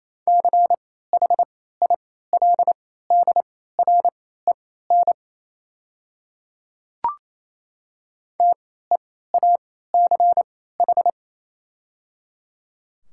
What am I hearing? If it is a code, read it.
Morse code, heard as a keyed tone, about 28 wpm, spaced as for 16 wpm, 700 Hz. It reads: CHILDREN TEACH
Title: 3x Character Spacing - Top 500 Words - 28wpm